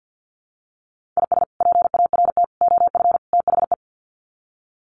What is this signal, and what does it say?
Morse code, heard as a keyed tone, about 48 wpm, 700 Hz. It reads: IS PART OF THE
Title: Sets of 4 Words 50wpm